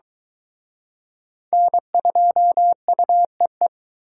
Morse code, heard as a keyed tone, about 23 wpm, 700 Hz. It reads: N2UEE